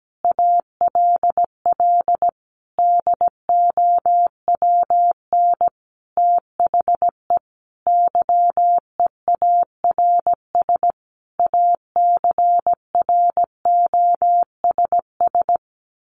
Morse code, heard as a keyed tone, about 17 wpm, 700 Hz. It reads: ALL DOWN THE YEARS ACROSS
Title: A Sound Of Thunder 17wpm